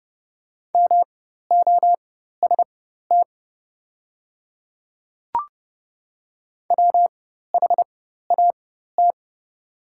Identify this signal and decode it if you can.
Morse code, heard as a keyed tone, about 30 wpm, spaced as for 14 wpm, 700 Hz. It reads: MOST WHAT
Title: 4x Character Spacing - Top 100 Words - 30wpm